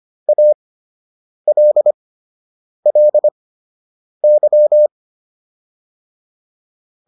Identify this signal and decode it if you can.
Morse code, heard as a keyed tone, about 25 wpm, spaced as for 8 wpm, 600 Hz. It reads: ALLY